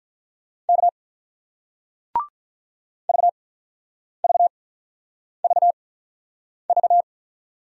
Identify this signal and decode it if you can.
Morse code, heard as a keyed tone, about 48 wpm, spaced as for 9 wpm, 700 Hz. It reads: K VVVV